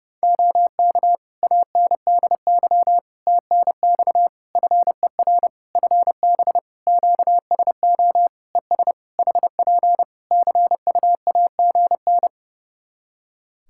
Morse code, heard as a keyed tone, about 32 wpm, 700 Hz. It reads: OK ANDY TNX FER FB QSO ES HP CUAGN
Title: K1USN SST QSO Practice 30wpm